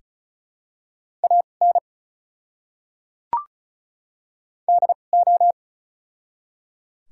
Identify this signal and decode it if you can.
Morse code, heard as a keyed tone, about 34 wpm, spaced as for 5 wpm, 700 Hz. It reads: AND O
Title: 2x Character Spacing - Top 100 Words - 35wpm